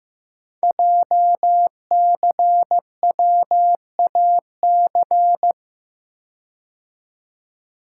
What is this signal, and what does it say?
Morse code, heard as a keyed tone, about 15 wpm, 700 Hz. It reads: JCWAC
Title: Code Groups - 5 Letters - 15wpm